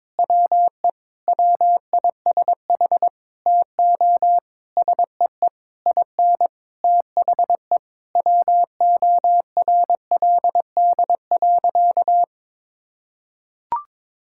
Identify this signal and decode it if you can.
Morse code, heard as a keyed tone, about 22 wpm, 700 Hz. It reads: WE WISH TO SEE IN THE WORLD.
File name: practice / 22wpm / Sentences with 8-Word Phrases - 22wpm